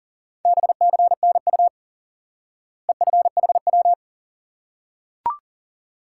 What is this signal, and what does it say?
Morse code, heard as a keyed tone, about 40 wpm, 700 Hz. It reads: BCNU EFHW
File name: IQE Course 3 - Lesson 89 - YAGI - Two Elements - 40wpm